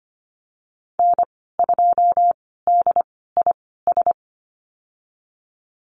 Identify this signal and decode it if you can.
Morse code, heard as a keyed tone, about 25 wpm, spaced as for 16 wpm, 700 Hz. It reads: N2DIS
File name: Lesson 15g - Number 5 - Call Signs 25wpm